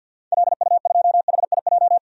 Morse code, heard as a keyed tone, about 52 wpm, 700 Hz. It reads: LU2HIJ
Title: DXCC-100 Call Signs - 6 Characters - 50wpm